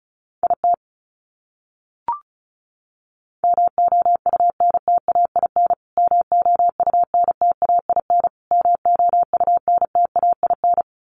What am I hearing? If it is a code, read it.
Morse code, heard as a keyed tone, about 34 wpm, 700 Hz. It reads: IT MOUNTAIN MOUNTAIN MOUNTAIN
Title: Top 500 Words Repeated 35wpm